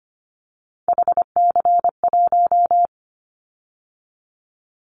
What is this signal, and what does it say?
Morse code, heard as a keyed tone, about 25 wpm, 700 Hz. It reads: HC1
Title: Callsign Prefixes for Top 100 DXCC - 25wpm